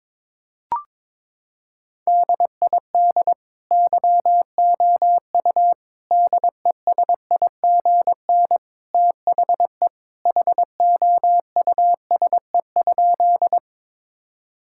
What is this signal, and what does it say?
Morse code, heard as a keyed tone, about 22 wpm, 700 Hz. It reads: DID YOU DESIGN THE HOUSE?
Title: Sentences from Top 1000 Words - Review Part 5 - 22wpm